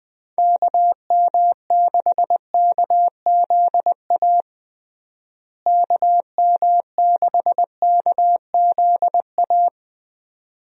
Morse code, heard as a keyed tone, about 20 wpm, 700 Hz. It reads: KM6KZA KM6KZA
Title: USA Call Signs - 6 Characters - Repeated 2x - 20wpm